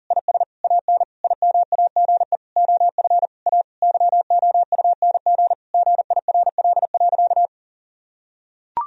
Morse code, heard as a keyed tone, about 40 wpm, 700 Hz. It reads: IS AN IMAGE OF A YOUNG GIRL.
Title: Sentences from Top 1000 Words - Review Part 4 - 40wpm